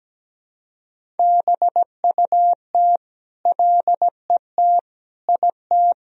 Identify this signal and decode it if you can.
Morse code, heard as a keyed tone, about 17 wpm, 700 Hz. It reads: BUT LET IT
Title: Sentences from Top 1000 Words - Review Part 1 - 17wpm